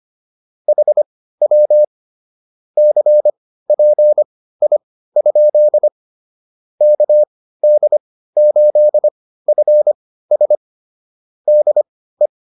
Morse code, heard as a keyed tone, about 25 wpm, spaced as for 15 wpm, 600 Hz. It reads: HW CPI? KD8FS DE
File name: QSO107 15